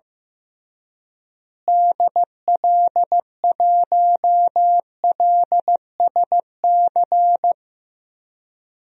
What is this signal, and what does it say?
Morse code, heard as a keyed tone, about 15 wpm, 700 Hz. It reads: DL1LSC
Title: DXCC-100 Call Signs - 6 Characters - Repeated 2x - 15wpm